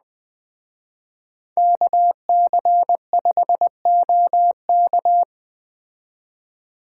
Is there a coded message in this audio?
Yes: Morse code from 1.6 to 5.2 s, heard as a keyed tone, about 20 wpm, 700 Hz. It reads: KC5OK